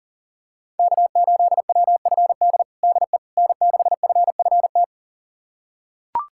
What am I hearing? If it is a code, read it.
Morse code, heard as a keyed tone, about 37 wpm, 700 Hz. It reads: K8WFD DE N6FFT
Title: USA Callsign DE Callsign - 5 Characters - 40wpm